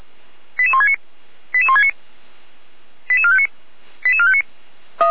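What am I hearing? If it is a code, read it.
Five-tone selective call (ZVEI), heard as quick runs of five tones, tones of about 70 ms. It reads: 88169 88169 88459 88459
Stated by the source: ZVEI five-tone selective call received on an SDR.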